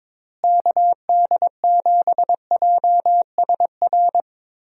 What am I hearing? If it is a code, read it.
Morse code, heard as a keyed tone, about 22 wpm, 700 Hz. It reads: KD7JSR
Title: USA Call Signs - 6 Characters - Repeated 3x - 22wpm